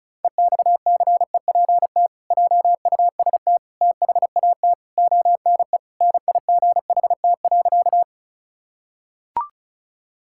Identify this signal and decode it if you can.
Morse code, heard as a keyed tone, about 34 wpm, 700 Hz. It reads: EXCEPT JUST THAT ONE NIGHT.
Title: Sentences from Top 1600 Words 35wpm